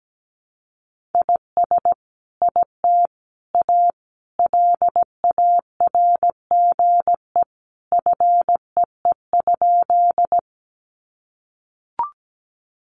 Morse code, heard as a keyed tone, about 17 wpm, 700 Hz. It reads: IS IT A LARGE FEE?
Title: Sentences from Top 1500 Words 17wpm